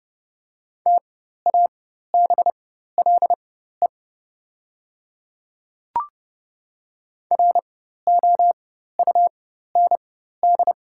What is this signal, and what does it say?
Morse code, heard as a keyed tone, about 30 wpm, spaced as for 14 wpm, 690 Hz. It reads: TABLE ROUND
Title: 4x Character Spacing - Top 500 Words - 30wpm